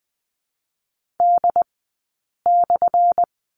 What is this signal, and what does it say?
Morse code, heard as a keyed tone, about 20 wpm, 700 Hz. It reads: D /
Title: 177 Lesson 29d - Slash - Two Characters 20wpm